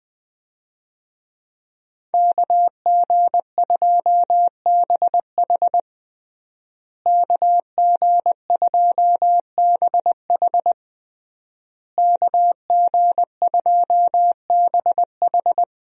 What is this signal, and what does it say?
Morse code, heard as a keyed tone, about 20 wpm, 700 Hz. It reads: KG2BH KG2BH KG2BH
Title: USA Call Signs - 5 Characters - Repeated 3x - 20wpm